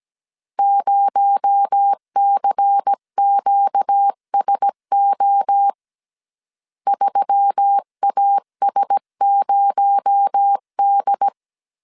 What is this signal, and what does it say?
Morse code, heard as a keyed tone, about 17 wpm, 800 Hz. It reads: 0CQSO 3AS0D
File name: CW / DET / 1 6 1500C S C 16wpm 2733